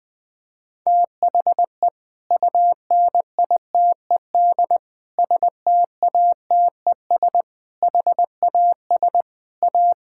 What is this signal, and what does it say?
Morse code, heard as a keyed tone, about 20 wpm, 690 Hz. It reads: THE UNITED STATES HAS A
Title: Sentences with 4-Word Phrases - Spoken Beginning - 20wpm